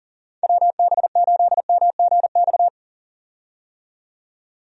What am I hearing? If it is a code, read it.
Morse code, heard as a keyed tone, about 40 wpm, 700 Hz. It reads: WB8MGX
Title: USA Call Signs - 6 Characters - 40wpm